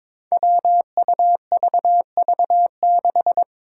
Morse code, heard as a keyed tone, about 22 wpm, 700 Hz. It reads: WUVV6
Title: Five Challenging Letter-Number Combinations 22wpm